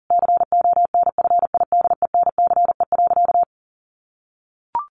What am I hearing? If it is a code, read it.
Morse code, heard as a keyed tone, about 40 wpm, 700 Hz. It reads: CONFIDENCE.
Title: Sentences from Top 1500 Words 40wpm